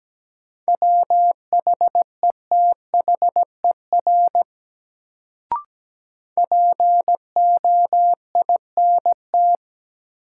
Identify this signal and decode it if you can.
Morse code, heard as a keyed tone, about 17 wpm, 700 Hz. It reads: WHETHER POINT
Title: Sets of 3 Words - New Word Drill - 17wpm